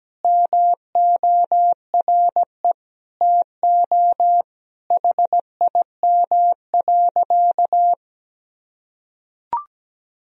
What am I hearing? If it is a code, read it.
Morse code, heard as a keyed tone, about 17 wpm, 700 Hz. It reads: MORE TO HIM.
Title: Sentences from Top 1000 Words - Review Part 1 - 17wpm